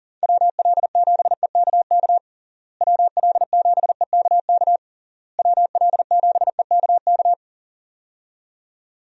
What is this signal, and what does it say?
Morse code, heard as a keyed tone, about 40 wpm, 700 Hz. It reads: WL7EKK WL7EKK WL7EKK
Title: USA Call Signs - 6 Characters - Repeated 3x - 40wpm